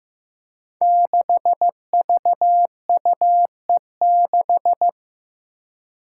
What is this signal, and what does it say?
Morse code, heard as a keyed tone, about 15 wpm, 700 Hz. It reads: 6VUE6
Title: Five Challenging Letter-Number Combinations 15wpm